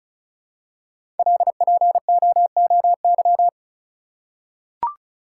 Morse code, heard as a keyed tone, about 34 wpm, 700 Hz. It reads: LPOOY
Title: Code Groups - 5 Letters - 35wpm